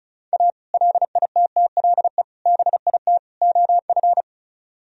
Morse code, heard as a keyed tone, about 43 wpm, 700 Hz. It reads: A LITTLE BIT OF
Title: Speed-Racing - Sets of 4 Words 35wpm